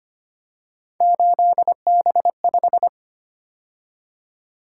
Morse code, heard as a keyed tone, about 25 wpm, 700 Hz. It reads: 8B5
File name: Three Letter-Number Combinations - 25wpm